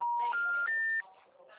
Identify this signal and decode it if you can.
Telephone call progress tone, heard as a telephone network's tone. Special information tone.